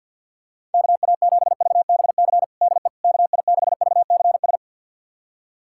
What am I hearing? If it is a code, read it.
Morse code, heard as a keyed tone, about 52 wpm, 700 Hz. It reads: KA7VBC DE KI6VCS
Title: USA Callsign DE Callsign - 6 Characters - 50wpm